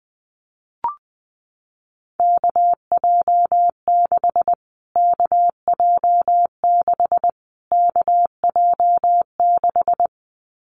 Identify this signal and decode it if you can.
Morse code, heard as a keyed tone, about 19 wpm, 700 Hz. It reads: KJ6 KJ6 KJ6
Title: Callsign Prefixes for Top 25 DXCC - Repeated 3x - 20wpm